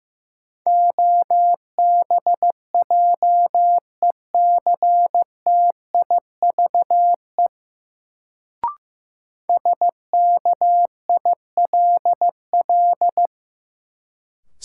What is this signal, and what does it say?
Morse code, heard as a keyed tone, about 15 wpm, 700 Hz. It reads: OBJECTIVE SKILL
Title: Sentences from Top 1200 Words - New Word Drill Alt - 15wpm